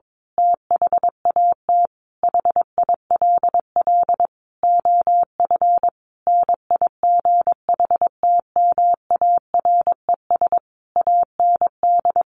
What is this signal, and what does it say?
Morse code, heard as a keyed tone, about 22 wpm, 700 Hz. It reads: THAT HILL OF NIGHTMARES AND
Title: A Sound Of Thunder 22wpm